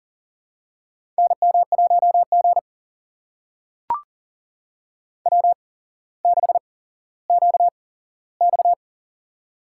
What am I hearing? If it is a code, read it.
Morse code, heard as a keyed tone, about 40 wpm, spaced as for 17 wpm, 700 Hz. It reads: NM1G W6QX
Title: USA Call Signs - 4 Characters - 8x Character Spacing - 40wpm